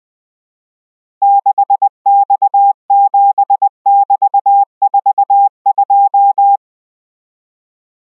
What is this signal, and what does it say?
Morse code, heard as a keyed tone, about 20 wpm, 800 Hz. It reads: 6X7<BT>42